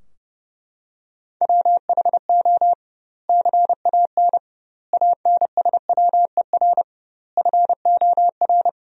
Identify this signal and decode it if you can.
Morse code, heard as a keyed tone, about 30 wpm, 700 Hz. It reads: WHO CAN ANSWER FOR
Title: 2x Word-Spacing - Sentences from Top 300 Words - 30wpm